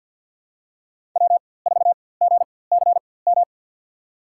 Morse code, heard as a keyed tone, about 53 wpm, spaced as for 24 wpm, 700 Hz. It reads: W4GCM
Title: USA Call Signs - 5 Characters - 4x Character Spacing - 50wpm